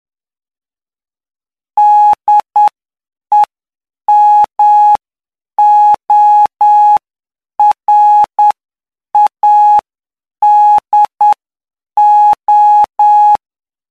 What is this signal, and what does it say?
Morse code, heard as a keyed tone, about 9 wpm, spaced as for 7 wpm, 840 Hz. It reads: DEMORADO
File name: cw / py2lcd / 05-PY2LCD-Li��o 14